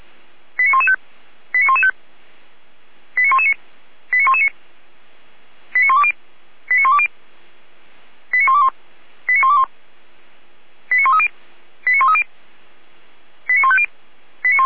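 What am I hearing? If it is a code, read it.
Five-tone selective call (ZVEI), heard as quick runs of five tones, tones of about 70 ms. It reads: 89115 89115 89119 89119 89120 89120 89121 89121 89130 89130 89150
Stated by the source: ZVEI five-tone selective call received by SDR, recorded by Feuerwehr Orlamünde (fire brigade).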